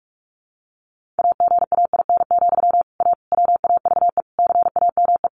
Morse code, heard as a keyed tone, about 44 wpm, 700 Hz. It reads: AGAIN, A WAVE CAME
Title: Sentences from Top 1400 Words 45wpm